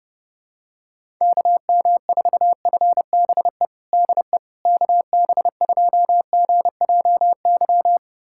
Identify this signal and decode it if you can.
Morse code, heard as a keyed tone, about 32 wpm, 700 Hz. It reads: KM4FBE DE KB2GJY
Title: USA Callsign DE Callsign - 6 Characters - 30wpm